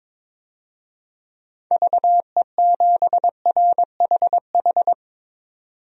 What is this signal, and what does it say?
Morse code, heard as a keyed tone, about 22 wpm, 700 Hz. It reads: VE7RHH
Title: DXCC-100 Call Signs - 6 Characters - 22wpm